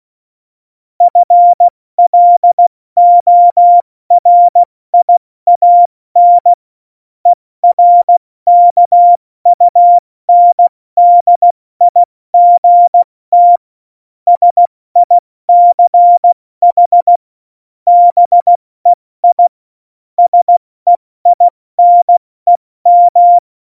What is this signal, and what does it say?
Morse code, heard as a keyed tone, about 16 wpm, 700 Hz. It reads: FLORIAN ERKUNDIGT SICH BEI SEINEM
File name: Morsekurs / Witze / Witz 20. August 2019, 70 F80